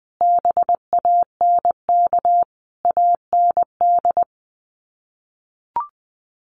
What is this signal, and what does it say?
Morse code, heard as a keyed tone, about 20 wpm, 700 Hz. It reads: BANK AND
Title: Sets of 5 Words Encore2 20wpm